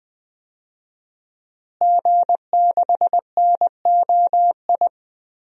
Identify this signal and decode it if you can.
Morse code, heard as a keyed tone, about 21 wpm, 700 Hz. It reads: G6NOI
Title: DXCC-100 Call Signs - 5 Characters - Repeated 3x - 20wpm